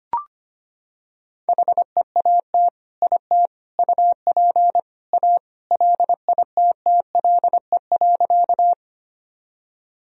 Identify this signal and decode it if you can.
Morse code, heard as a keyed tone, about 25 wpm, 700 Hz. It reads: HEAT IT UP A LITTLE.
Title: Sentences from Top 1000 Words - Review Part 3 - 25wpm